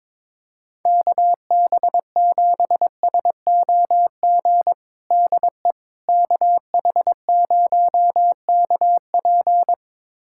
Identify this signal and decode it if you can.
Morse code, heard as a keyed tone, about 21 wpm, 700 Hz. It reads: KB7SOG DE KH0KP
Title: USA Callsign DE Callsign 22wpm